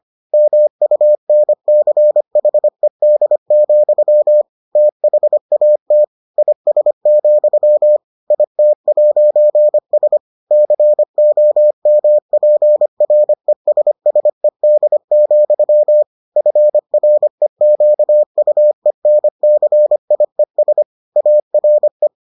Morse code, heard as a keyed tone, about 25 wpm, 600 Hz. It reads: MUNCHED, THAT IS, IT'S COMPRESSED, FREQUENCIES ARE